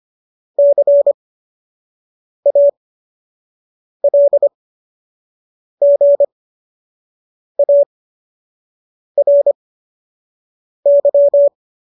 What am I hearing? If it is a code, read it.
Morse code, heard as a keyed tone, about 25 wpm, spaced as for 6 wpm, 570 Hz. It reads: CALGARY
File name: qso002 06